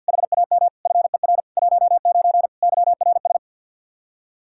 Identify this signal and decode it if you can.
Morse code, heard as a keyed tone, about 49 wpm, 700 Hz. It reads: HAM FER 19 YRS
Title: Practice QSO - Exchange 3 - More Info - 50wpm